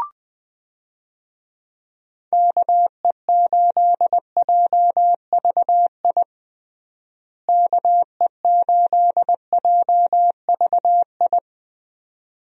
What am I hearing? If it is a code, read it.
Morse code, heard as a keyed tone, about 20 wpm, 700 Hz. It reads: KE8JVI KE8JVI
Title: USA Call Signs - 6 Characters - Repeated 2x - 20wpm